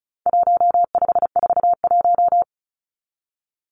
Morse code, heard as a keyed tone, about 34 wpm, 700 Hz. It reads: 1541
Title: Numbers - 4-Digit 35wpm